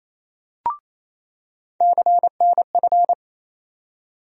Morse code, heard as a keyed tone, about 28 wpm, 700 Hz. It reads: CNF
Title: USA Call Sign Postfixes - 3 Letters 28wpm